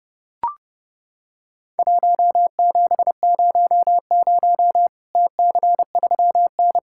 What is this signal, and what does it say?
Morse code, heard as a keyed tone, about 30 wpm, 700 Hz. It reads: 1700 TC3N
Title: Worked All Europe DX Contest - QTC Exchange - 30wpm